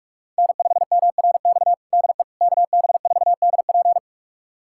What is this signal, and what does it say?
Morse code, heard as a keyed tone, about 46 wpm, 700 Hz. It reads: N5MRX DE KB4DP
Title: USA Callsign DE Callsign - 5 Characters - 45wpm